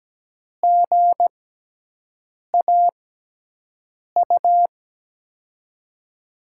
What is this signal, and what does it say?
Morse code, heard as a keyed tone, about 17 wpm, spaced as for 6 wpm, 700 Hz. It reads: GAU
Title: USA Call Signs - 6x Character Spacing - 17wpm